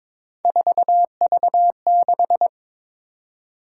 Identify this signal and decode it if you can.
Morse code, heard as a keyed tone, about 22 wpm, 700 Hz. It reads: 4V6
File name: practice / 22wpm / Three Letter-Number Combinations - 22wpm